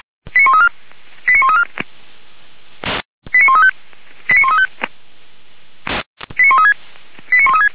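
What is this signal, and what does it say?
Five-tone selective call (ZVEI), heard as quick runs of five tones, tones of about 70 ms. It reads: 89134 89134 89135 89135 89136
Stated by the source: ZVEI five-tone call received by SDR.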